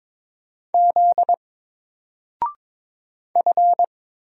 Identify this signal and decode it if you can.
Morse code, heard as a keyed tone, about 22 wpm, 700 Hz. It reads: Z F
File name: Single-Letter-Number-22wpm